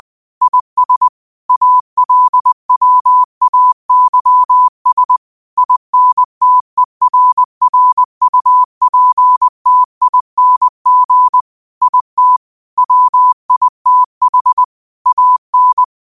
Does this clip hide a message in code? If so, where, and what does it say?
Yes: Morse code from 0.4 to 15.8 s, heard as a keyed tone, about 20 wpm, 1000 Hz. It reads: IS ALWAYS INTERRUPTING IT WITH AN